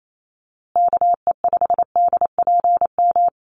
Morse code, heard as a keyed tone, about 28 wpm, 700 Hz. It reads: KE5DPM
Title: USA Call Signs Repeated 2x 28wpm